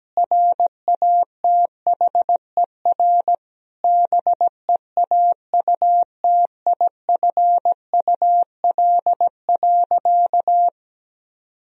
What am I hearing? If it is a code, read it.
Morse code, heard as a keyed tone, about 17 wpm, 700 Hz. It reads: RATHER BEAUTIFUL.
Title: Sentences from Top 1000 Words - Review Part 4 - 17wpm